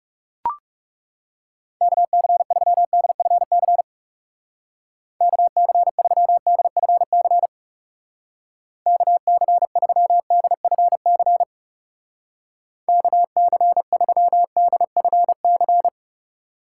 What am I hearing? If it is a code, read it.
Morse code, heard as a keyed tone, about 37 wpm, 700 Hz. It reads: KC3DFC KC3DFC KC3DFC KC3DFC
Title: Speed Racing - USA 2x3 Call Signs 30wpm